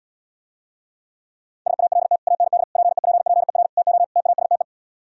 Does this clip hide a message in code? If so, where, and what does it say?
Yes: Morse code from 1.7 to 4.6 s, heard as a keyed tone, about 77 wpm, 700 Hz. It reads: SALT AND PEPPER TO TASTE